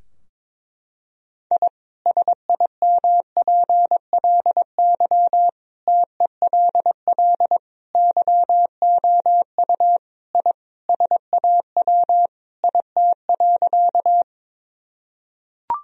Morse code, heard as a keyed tone, about 22 wpm, 700 Hz. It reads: I SIMPLY TELL YOU I SAW IT.
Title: Sentences from Top 1000 Words - Review Part 4 - 22wpm